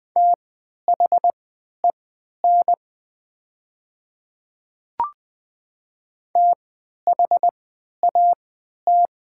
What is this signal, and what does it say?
Morse code, heard as a keyed tone, about 20 wpm, spaced as for 11 wpm, 700 Hz. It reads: THEN THAT
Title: 3x Character Spacing - Top 100 Words - 20wpm